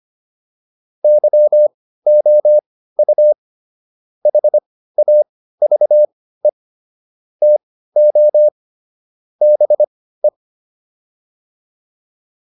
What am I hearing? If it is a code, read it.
Morse code, heard as a keyed tone, about 25 wpm, spaced as for 15 wpm, 600 Hz. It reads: YOU HAVE TO BE